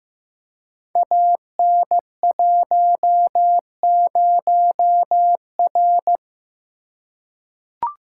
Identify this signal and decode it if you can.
Morse code, heard as a keyed tone, about 15 wpm, 700 Hz. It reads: AN10R
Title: Code Groups - 5 Letter:Numbers - 15wpm